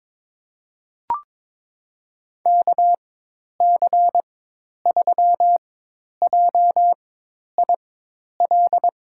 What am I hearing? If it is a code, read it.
Morse code, heard as a keyed tone, about 23 wpm, spaced as for 10 wpm, 700 Hz. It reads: KC3JIL
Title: USA Call Signs - 4x Character Spacing - 22wpm